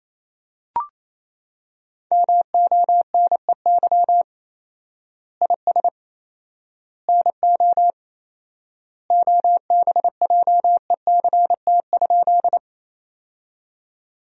Morse code, heard as a keyed tone, about 27 wpm, 700 Hz. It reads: MONEY IS NO OBJECT?
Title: 4x Word-Spacing - Sentences from Top 500 Words - 28wpm